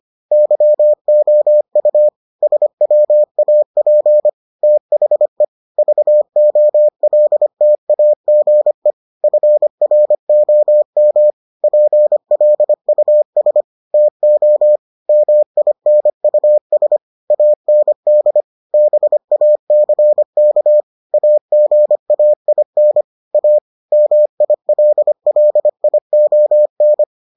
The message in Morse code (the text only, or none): YOU SWAP THE VOLTAGE FROM PLUS TO MINUS AND BACK AGAIN A MILLION